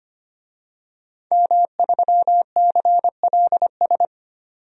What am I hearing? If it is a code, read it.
Morse code, heard as a keyed tone, about 25 wpm, 700 Hz. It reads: M3CLS
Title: DXCC-100 Call Signs - 5 Characters - 25wpm